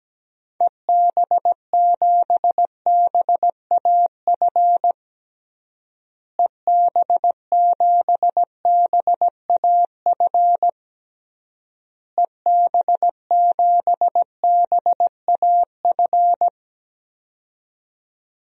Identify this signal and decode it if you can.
Morse code, heard as a keyed tone, about 17 wpm, 700 Hz. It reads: EB7BAF EB7BAF EB7BAF
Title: DXCC-100 Call Signs - 6 Characters - Repeated 3x - 17wpm